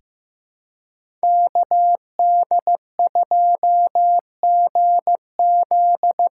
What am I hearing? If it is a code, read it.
Morse code, heard as a keyed tone, about 15 wpm, 700 Hz. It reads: KD2GZ
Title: USA Call Signs - 6 Characters - 6x Character Spacing - 15wpm